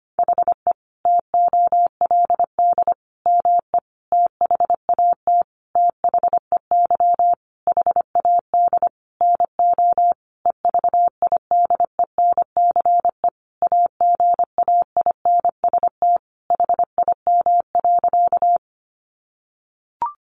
Morse code, heard as a keyed tone, about 25 wpm, 700 Hz. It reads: HE TOLD ME THAT THEY HAD NO EVIDENCE AGAINST HIM.
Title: Sentences from Top 1300 Words 25wpm